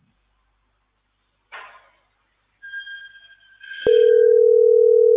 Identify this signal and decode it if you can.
Ringback tone (North America), 440 + 480 Hz.